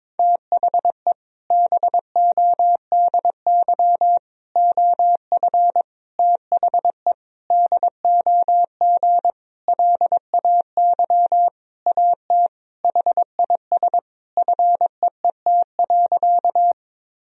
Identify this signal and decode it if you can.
Morse code, heard as a keyed tone, about 22 wpm, 700 Hz. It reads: THE BODY OF THE DOG LAY AT HIS FEET.
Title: Sentences from Top 1000 Words - Review Part 2 - 22wpm